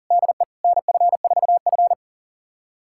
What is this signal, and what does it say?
Morse code, heard as a keyed tone, about 40 wpm, 700 Hz. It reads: DE NF4F
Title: USA Callsign DE Callsign - 4 Characters - 40wpm